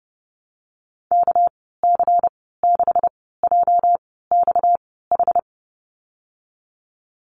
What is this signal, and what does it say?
Morse code, heard as a keyed tone, about 30 wpm, spaced as for 17 wpm, 700 Hz. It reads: KC6JXH
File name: Lesson 37g - Letter x - Call Signs 30wpm